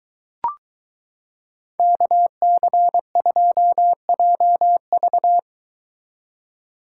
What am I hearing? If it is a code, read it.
Morse code, heard as a keyed tone, about 23 wpm, 700 Hz. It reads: KC2JV